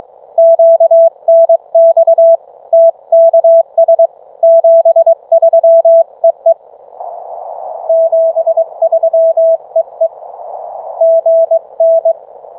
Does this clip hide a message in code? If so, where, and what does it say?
Yes: Morse code from 0.4 to 12.1 s, heard as a keyed tone, about 20 wpm, 660 Hz. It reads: QNX TKS 73 EE 73 EE GN